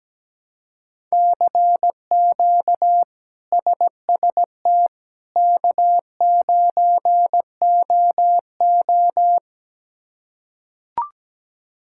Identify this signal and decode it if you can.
Morse code, heard as a keyed tone, about 17 wpm, 700 Hz. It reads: CQ SST K9OO
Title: K1USN SST QSO Practice - 17wpm